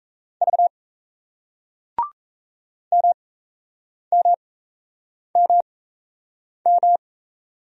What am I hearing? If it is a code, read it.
Morse code, heard as a keyed tone, about 38 wpm, spaced as for 8 wpm, 700 Hz. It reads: V MMMM